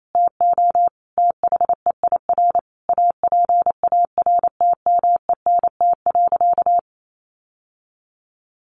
Morse code, heard as a keyed tone, about 28 wpm, 700 Hz. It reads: TO THEIR APARTMENT.
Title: Sentences from Top 900 Words 28wpm